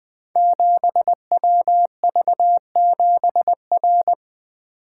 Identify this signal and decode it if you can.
Morse code, heard as a keyed tone, about 20 wpm, 700 Hz. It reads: 7WV7R
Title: Code Groups - 5 Letter:Numbers - 20wpm